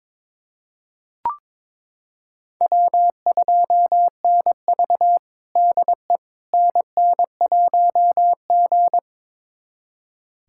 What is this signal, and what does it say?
Morse code, heard as a keyed tone, about 22 wpm, 700 Hz. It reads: W2NV DE NN1G
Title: USA Callsign DE Callsign - 4 Characters - 22wpm